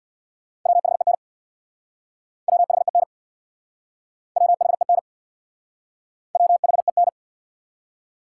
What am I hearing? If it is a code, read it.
Morse code, heard as a keyed tone, about 64 wpm, 700 Hz. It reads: WHEN WHEN WHEN WHEN